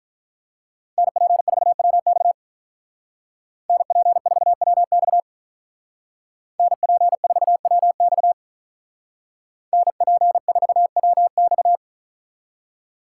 Morse code, heard as a keyed tone, about 42 wpm, 700 Hz. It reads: NP4WX NP4WX NP4WX NP4WX
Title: Speed Racing - USA 2x2 Call Signs 35wpm